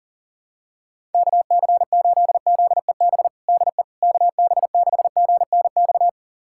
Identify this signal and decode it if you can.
Morse code, heard as a keyed tone, about 40 wpm, 700 Hz. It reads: KC8ZEB DE KB6GNX
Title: USA Callsign DE Callsign 40wpm